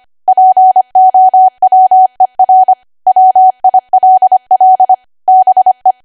Morse code, heard as a keyed tone, about 25 wpm, 750 Hz. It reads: POWER WILL BE
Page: 25 WPM Code Archive